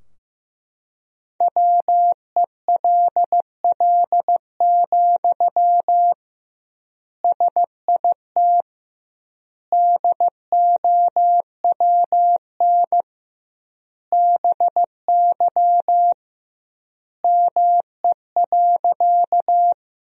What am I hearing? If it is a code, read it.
Morse code, heard as a keyed tone, about 15 wpm, 700 Hz. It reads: WELL, SIT DOWN BY ME.
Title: 2x Word-Spacing - Sentences from Top 500 Words - 15wpm